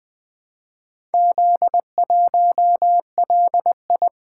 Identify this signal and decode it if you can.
Morse code, heard as a keyed tone, about 20 wpm, 700 Hz. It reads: Z1LI
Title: Four Letter-Number Combinations - 20wpm